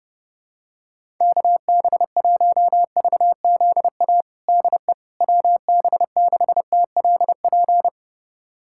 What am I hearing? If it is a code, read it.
Morse code, heard as a keyed tone, about 32 wpm, 700 Hz. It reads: KB1VZA DE WB6TLP
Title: USA Callsign DE Callsign - 6 Characters - 30wpm